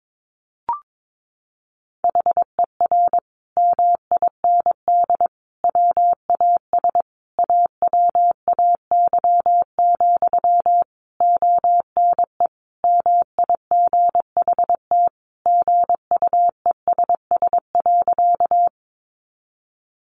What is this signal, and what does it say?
Morse code, heard as a keyed tone, about 22 wpm, 700 Hz. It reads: HER MIND WAS AWAY, ONE MIGHT GUESS.
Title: Sentences from Top 1500 Words 22wpm